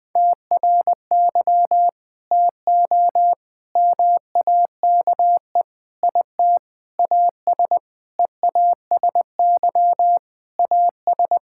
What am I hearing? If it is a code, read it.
Morse code, heard as a keyed tone, about 21 wpm, 700 Hz. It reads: TRY TO MAKE IT AS EASY AS
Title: Sentences with 8-Word Phrases - 20wpm